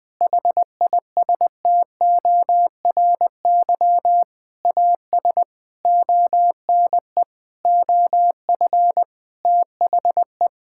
Morse code, heard as a keyed tone, about 20 wpm, 700 Hz. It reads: HISTORY AS ONE OF THE
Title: Sentences with 8-Word Phrases - 20wpm